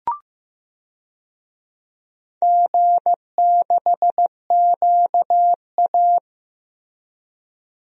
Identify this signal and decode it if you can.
Morse code, heard as a keyed tone, about 15 wpm, 700 Hz. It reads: G6QA
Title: DXCC-100 Call Signs - 4 Characters - Repeated 2x - 15wpm